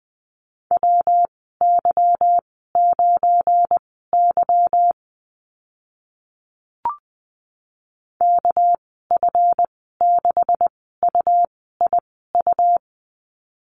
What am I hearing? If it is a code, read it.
Morse code, heard as a keyed tone, about 20 wpm, spaced as for 14 wpm, 700 Hz. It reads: WY9Y KF6UIU
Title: 238 Lesson 38g - Letter q - Call Signs 20wpm